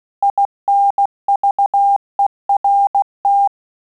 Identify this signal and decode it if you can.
Morse code, heard as a keyed tone, about 16 wpm, 800 Hz. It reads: INVERT
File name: QSO CW 75 CPM